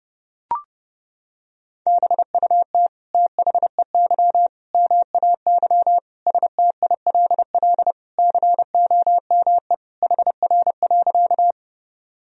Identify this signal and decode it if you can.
Morse code, heard as a keyed tone, about 30 wpm, 700 Hz. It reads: BUT THEY MAY STILL COME HR.
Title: Sentences from Top 300 Words and Abbreviations 30wpm